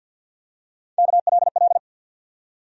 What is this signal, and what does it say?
Morse code, heard as a keyed tone, about 47 wpm, 700 Hz. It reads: KLL
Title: USA Call Sign Postfixes - 3 Letters 50wpm